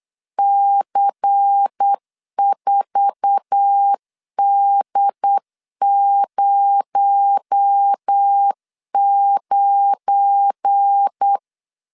Morse code, heard as a keyed tone, about 8 wpm, 800 Hz. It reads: C4D09
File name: LEZIONI / 8 YJ45 / 8 YJ4 8wpm 3246